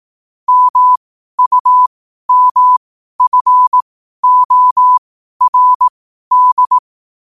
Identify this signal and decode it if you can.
Morse code, heard as a keyed tone, about 18 wpm, spaced as for 12 wpm, 1000 Hz. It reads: MUMFORD
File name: qotdfiles / 12wpm